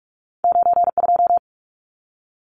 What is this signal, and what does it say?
Morse code, heard as a keyed tone, about 46 wpm, 700 Hz. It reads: 92
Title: Numbers - 2-Digit 45wpm